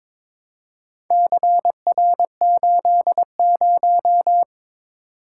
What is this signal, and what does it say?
Morse code, heard as a keyed tone, about 22 wpm, 700 Hz. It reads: CR80
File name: practice / 22wpm / Four Letter-Number Combinations - 22wpm